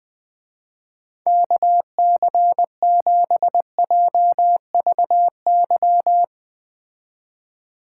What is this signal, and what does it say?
Morse code, heard as a keyed tone, about 20 wpm, 700 Hz. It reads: KC7JVY
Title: Speed Racing - USA 2x3 Call Signs 17wpm